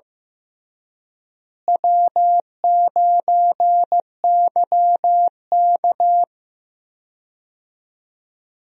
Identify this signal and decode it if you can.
Morse code, heard as a keyed tone, about 15 wpm, 700 Hz. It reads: W9YK
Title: USA Call Signs - 4 Characters - Repeated 3x - 15wpm